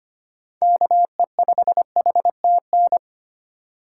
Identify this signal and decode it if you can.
Morse code, heard as a keyed tone, about 25 wpm, 700 Hz. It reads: KE5HTN
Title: USA Call Signs - 4x Character Spacing - 25wpm